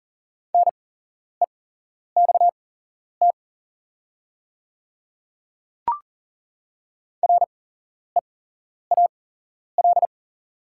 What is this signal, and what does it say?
Morse code, heard as a keyed tone, about 37 wpm, spaced as for 11 wpm, 690 Hz. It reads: NEXT REAL
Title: 8x Character Spacing - Top 300 Words - 40wpm